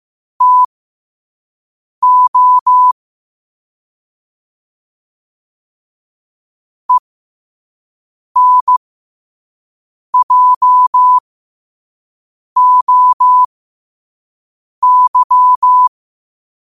Morse code, heard as a keyed tone, about 15 wpm, spaced as for 5 wpm, 1000 Hz. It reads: TO ENJOY
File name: qotdfiles / 5wpm